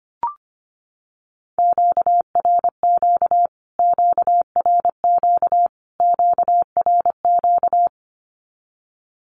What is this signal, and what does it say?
Morse code, heard as a keyed tone, about 25 wpm, 700 Hz. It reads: QRQ QRQ QRQ
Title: Q-Codes Repeated 3x 25wpm